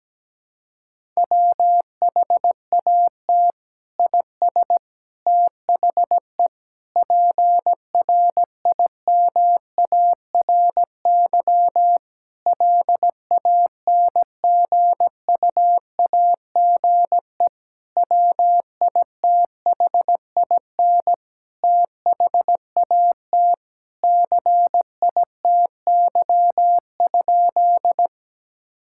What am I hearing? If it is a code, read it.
Morse code, heard as a keyed tone, about 17 wpm, 700 Hz. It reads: WHAT IS THE PRIMARY LANGUAGE WITHIN THAT CITY?
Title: Sentences from Top 1000 Words - Review Part 5 - 17wpm